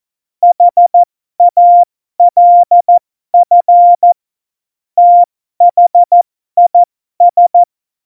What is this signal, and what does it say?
Morse code, heard as a keyed tone, about 14 wpm, spaced as for 12 wpm, 700 Hz. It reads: HALF THIS
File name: Morsekurs / Witze